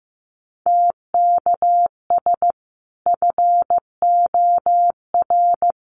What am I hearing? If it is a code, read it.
Morse code, heard as a keyed tone, about 15 wpm, 700 Hz. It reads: TKS FOR
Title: Common QSO Phrases 15wpm